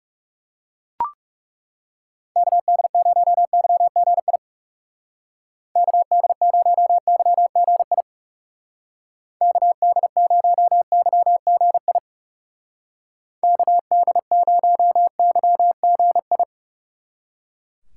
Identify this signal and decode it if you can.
Morse code, heard as a keyed tone, about 40 wpm, 700 Hz. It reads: KD0YGI KD0YGI KD0YGI KD0YGI